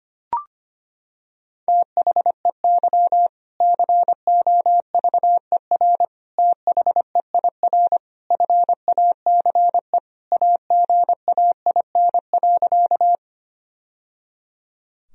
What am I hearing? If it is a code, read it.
Morse code, heard as a keyed tone, about 25 wpm, 700 Hz. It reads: THEY COVER THEIR FACE AGAIN.
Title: Sentences from Top 1000 Words - Review Part 1 - 25wpm